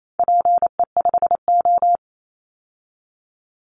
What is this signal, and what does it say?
Morse code, heard as a keyed tone, about 28 wpm, 700 Hz. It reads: PE5O
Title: Call Signs - 28wpm